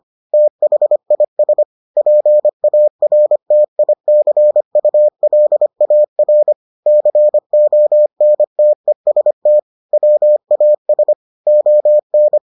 Morse code, heard as a keyed tone, about 25 wpm, 600 Hz. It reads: THIS PARTICULAR CONTEST WAS ON